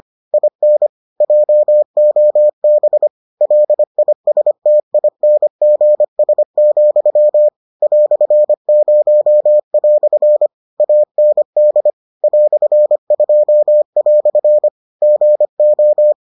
Morse code, heard as a keyed tone, about 25 wpm, 600 Hz. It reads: IN JOB LISTINGS, "0" AND "2" GO